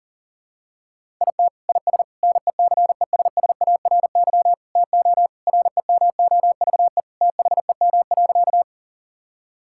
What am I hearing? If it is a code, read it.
Morse code, heard as a keyed tone, about 40 wpm, 700 Hz. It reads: IT IS NECESSARY TO REMOVE THEM.
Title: Sentences from Top 1000 Words - Review Part 5 - 40wpm